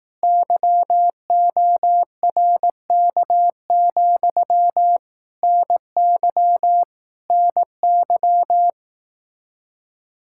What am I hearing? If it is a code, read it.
Morse code, heard as a keyed tone, about 18 wpm, 700 Hz. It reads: YORK, NY NY
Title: Speed-Racing Common QSO Phrases Encore - 15wpm